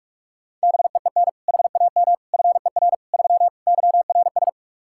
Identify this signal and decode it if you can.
Morse code, heard as a keyed tone, about 46 wpm, 700 Hz. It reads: BEEN HAM FER 3 YRS
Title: K1USN SST QSO Practice 45wpm